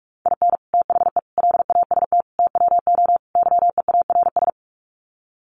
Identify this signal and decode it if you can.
Morse code, heard as a keyed tone, about 44 wpm, 700 Hz. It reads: IN THE LAST TWO YEARS
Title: Sets of 5 Words 45wpm